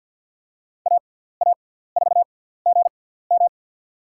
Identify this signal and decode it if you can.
Morse code, heard as a keyed tone, about 50 wpm, spaced as for 17 wpm, 700 Hz. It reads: AA4GM
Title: USA Call Signs - 6x Character Spacing - 50wpm